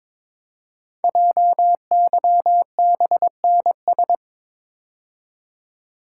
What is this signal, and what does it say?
Morse code, heard as a keyed tone, about 22 wpm, 700 Hz. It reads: JYBNS
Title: Code Groups - 5 Letters - 22wpm